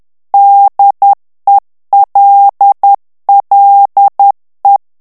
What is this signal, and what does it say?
Morse code, heard as a keyed tone, about 10 wpm, 800 Hz. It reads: DELLE
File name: ITALIEN texte clair 2 vitesse 10 mots minute